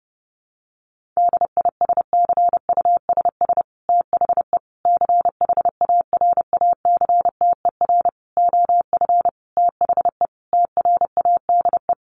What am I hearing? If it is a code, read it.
Morse code, heard as a keyed tone, about 30 wpm, 700 Hz. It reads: DISCUSS THE CHARACTER OF THE TRADE
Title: Sentences from Top 1500 Words 30wpm